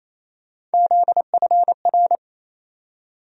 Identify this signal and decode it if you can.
Morse code, heard as a keyed tone, about 28 wpm, 700 Hz. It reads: ZFR
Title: USA Call Sign Postfixes - 3 Letters 28wpm